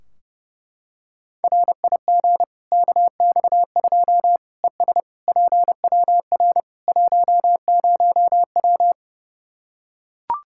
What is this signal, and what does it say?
Morse code, heard as a keyed tone, about 30 wpm, 700 Hz. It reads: RIG KX2 ES PWR 10W
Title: K1USN SST QSO Practice 30wpm